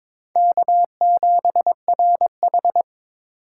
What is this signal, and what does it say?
Morse code, heard as a keyed tone, about 21 wpm, 700 Hz. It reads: K7RH
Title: USA Call Signs - 4 Characters - 4x Character Spacing - 22wpm